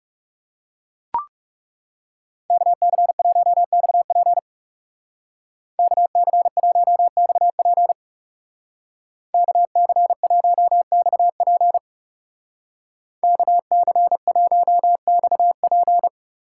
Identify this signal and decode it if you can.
Morse code, heard as a keyed tone, about 36 wpm, 700 Hz. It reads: KC1XP KC1XP KC1XP KC1XP
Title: Speed Racing - USA 2x2 Call Signs 30wpm